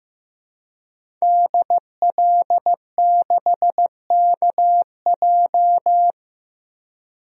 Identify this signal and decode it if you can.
Morse code, heard as a keyed tone, about 15 wpm, 700 Hz. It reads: DL6KJ
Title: DXCC-100 Call Signs - 5 Characters - 15wpm